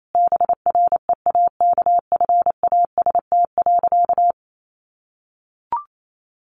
Morse code, heard as a keyed tone, about 28 wpm, 700 Hz. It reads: BREAKFAST.
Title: Sentences from Top 1500 Words 28wpm